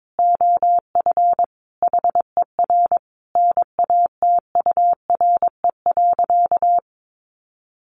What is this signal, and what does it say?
Morse code, heard as a keyed tone, about 22 wpm, 700 Hz. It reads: OF HER NATURE.
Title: Sentences from Top 1300 Words 22wpm